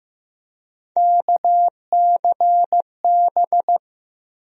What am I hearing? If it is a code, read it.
Morse code, heard as a keyed tone, about 15 wpm, 700 Hz. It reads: KCB
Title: USA Call Sign Postfixes - 3 Letters 15wpm